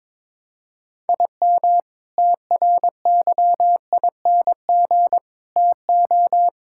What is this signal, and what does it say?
Morse code, heard as a keyed tone, about 22 wpm, 700 Hz. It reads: IM TRYING TO
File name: practice / 22wpm / Sentences with 4-Word Phrases - 22wpm